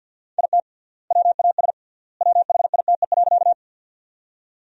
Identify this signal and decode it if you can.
Morse code, heard as a keyed tone, about 49 wpm, 700 Hz. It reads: IT WAS WHITE.
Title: 3x Word-Spacing - Sentences from Top 300 Words - 50wpm